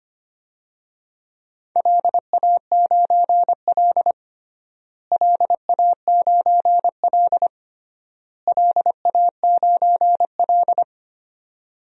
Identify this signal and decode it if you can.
Morse code, heard as a keyed tone, about 25 wpm, 700 Hz. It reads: LA9L LA9L LA9L
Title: DXCC-100 Call Signs - 4 Characters - Repeated 3x - 25wpm